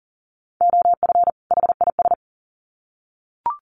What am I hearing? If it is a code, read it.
Morse code, heard as a keyed tone, about 40 wpm, 700 Hz. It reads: OF HIS
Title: Sets of 2 Words 40wpm